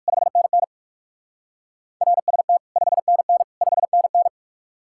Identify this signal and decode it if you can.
Morse code, heard as a keyed tone, about 62 wpm, spaced as for 55 wpm, 700 Hz. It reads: 5NN RST 5NN 5NN
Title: Speed-Racing Common QSO Phrases - 45wpm